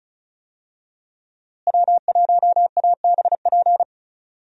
Morse code, heard as a keyed tone, about 34 wpm, 700 Hz. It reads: W1ABP
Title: USA Call Signs - 5 Characters - Repeated 2x - 35wpm